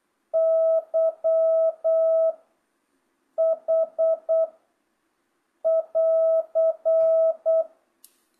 Morse code, heard as a keyed tone, about 8 wpm, 660 Hz. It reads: Y H <AR>
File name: jh8cbh / 2025cw